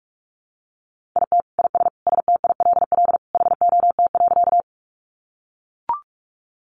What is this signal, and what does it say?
Morse code, heard as a keyed tone, about 44 wpm, 700 Hz. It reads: IT IS STILL HOT.
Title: Easy Phrases 45wpm